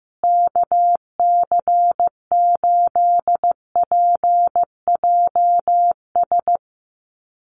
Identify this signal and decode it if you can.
Morse code, heard as a keyed tone, about 14 wpm, 700 Hz. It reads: KC8PJS
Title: USA Call Signs Repeated 3x 15wpm